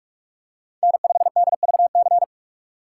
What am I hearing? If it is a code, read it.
Morse code, heard as a keyed tone, about 42 wpm, 700 Hz. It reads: N5DVC
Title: USA Call Signs - 2x Character Spacing - 45wpm